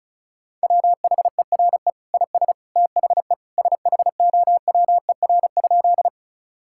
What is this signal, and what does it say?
Morse code, heard as a keyed tone, about 36 wpm, 700 Hz. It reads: WHERE IS THE SHOWER?
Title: Sentences from Top 1600 Words 35wpm